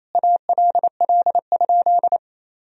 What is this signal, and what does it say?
Morse code, heard as a keyed tone, about 28 wpm, 700 Hz. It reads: ALL?
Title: Sentences from Top 1000 Words - Review Part 2 - 28wpm